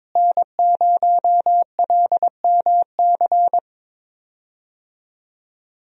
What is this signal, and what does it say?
Morse code, heard as a keyed tone, about 21 wpm, 700 Hz. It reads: N0LMC
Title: USA Call Signs - 5 Characters - Repeated 3x - 22wpm